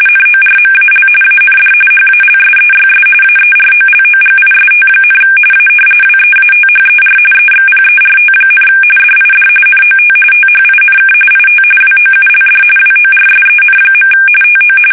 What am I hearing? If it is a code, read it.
RTTY, about 45.45 baud. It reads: NORMAL AND HALSEYS AEROLOGIST, A WEATHER EXPERT, DID NOT THINK IT WOULD WORSEN. TO
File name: fsk850 / battleshipnj